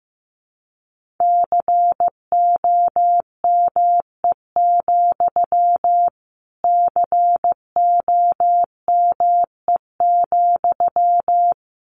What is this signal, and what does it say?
Morse code, heard as a keyed tone, about 15 wpm, 700 Hz. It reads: COME, COME,
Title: Sentences from Top 1500 Words 15wpm